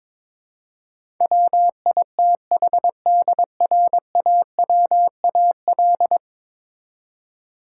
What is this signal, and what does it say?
Morse code, heard as a keyed tone, about 22 wpm, 700 Hz. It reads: WITHDRAWAL
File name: practice / 22wpm / 10 Letter Words - First 3 Letters Given - 22wpm